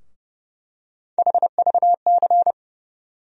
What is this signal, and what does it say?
Morse code, heard as a keyed tone, about 30 wpm, 700 Hz. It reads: HVC
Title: USA Call Sign Postfixes - 3 Letters 30wpm